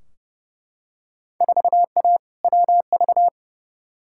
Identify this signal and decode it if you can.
Morse code, heard as a keyed tone, about 30 wpm, 700 Hz. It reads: 4A WV
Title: Field Day Exchanges Encore - 30wpm